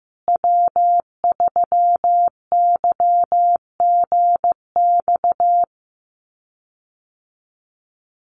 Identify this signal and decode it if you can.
Morse code, heard as a keyed tone, about 15 wpm, 700 Hz. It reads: W3YGX
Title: USA Call Signs Repeated 3x 15wpm